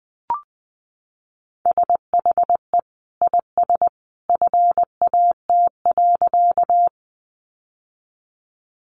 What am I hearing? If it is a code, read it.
Morse code, heard as a keyed tone, about 19 wpm, 700 Hz. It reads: SHE IS FAT.
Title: Sentences from Top 600 Words 20wpm